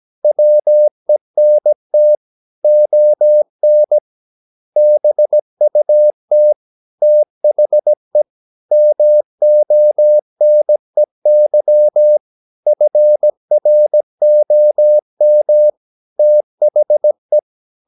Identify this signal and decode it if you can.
Morse code, heard as a keyed tone, about 17 wpm, 600 Hz. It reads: WENT ON BUT THE MONEY FROM THE
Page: Romanen Never af Ken Follet på engelsk.